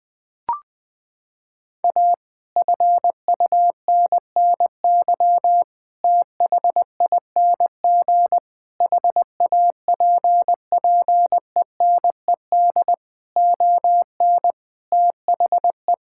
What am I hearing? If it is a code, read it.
Morse code, heard as a keyed tone, about 20 wpm, 700 Hz. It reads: A FUNNY THING HAPPENED ON THE
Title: Sets of 8 Words - 20wpm